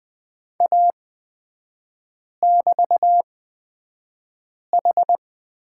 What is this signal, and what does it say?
Morse code, heard as a keyed tone, about 20 wpm, spaced as for 5 wpm, 690 Hz. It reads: A<BT>H
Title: 255 Lesson 41e - Period - Three Characters 20wpm